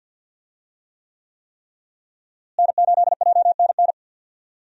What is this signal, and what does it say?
Morse code, heard as a keyed tone, about 51 wpm, 700 Hz. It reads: N8JNN
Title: USA Call Signs - 5 Characters - Repeated 3x - 50wpm